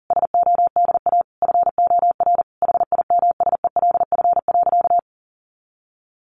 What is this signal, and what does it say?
Morse code, heard as a keyed tone, about 40 wpm, 700 Hz. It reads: SODA FOR HIMSELF.
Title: Additional Easy Phrases 40wpm